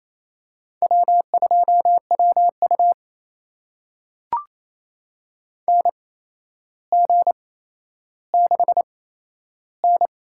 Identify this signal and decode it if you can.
Morse code, heard as a keyed tone, about 28 wpm, spaced as for 12 wpm, 700 Hz. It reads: W2WU NG6N
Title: USA Call Signs - 8x Character Spacing - 28wpm